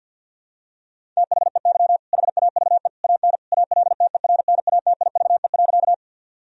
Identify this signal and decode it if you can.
Morse code, heard as a keyed tone, about 50 wpm, 700 Hz. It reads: THEY HAVE AN ALTERNATIVE.
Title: Sentences from Top 1000 Words - Review Part 4 - 50wpm